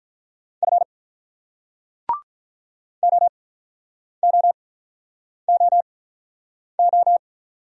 Morse code, heard as a keyed tone, about 46 wpm, spaced as for 9 wpm, 700 Hz. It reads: F OOOO